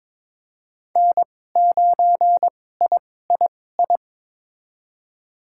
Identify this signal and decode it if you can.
Morse code, heard as a keyed tone, about 22 wpm, spaced as for 16 wpm, 700 Hz. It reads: N9III
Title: USA Call Signs - 2x Character Spacing - 22wpm